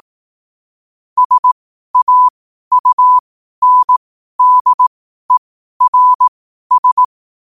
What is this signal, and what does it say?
Morse code, heard as a keyed tone, about 18 wpm, spaced as for 12 wpm, 1000 Hz. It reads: SAUNDERS
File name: qotdfiles / 12wpm